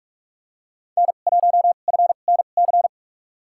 Morse code, heard as a keyed tone, about 48 wpm, spaced as for 33 wpm, 690 Hz. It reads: N1FNC
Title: USA Call Signs - 5 Characters - 2x Character Spacing - 45wpm